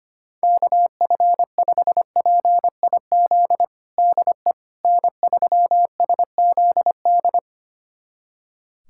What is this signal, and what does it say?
Morse code, heard as a keyed tone, about 25 wpm, 700 Hz. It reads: KF5PIZ DE N3SZD
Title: USA Callsign DE Callsign 25wpm